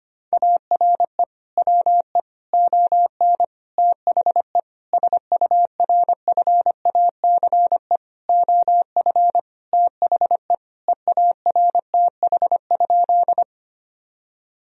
Morse code, heard as a keyed tone, about 25 wpm, 700 Hz. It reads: ARE WE ON THE SURFACE OF THE EARTH?
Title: Sentences from Top 1000 Words - Review Part 3 - 25wpm